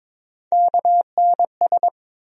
Morse code, heard as a keyed tone, about 22 wpm, 700 Hz. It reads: KNS
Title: USA Call Sign Postfixes - 3 Letters 22wpm